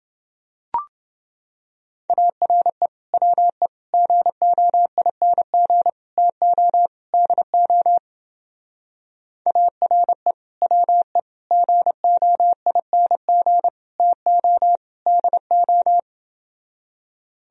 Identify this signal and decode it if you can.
Morse code, heard as a keyed tone, about 28 wpm, 700 Hz. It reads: ARE WE GOING TO DO ARE WE GOING TO DO